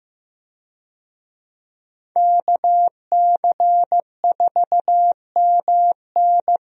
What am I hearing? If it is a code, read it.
Morse code, heard as a keyed tone, about 15 wpm, 700 Hz. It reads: KC4MN
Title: USA Call Signs - 5 Characters - 15wpm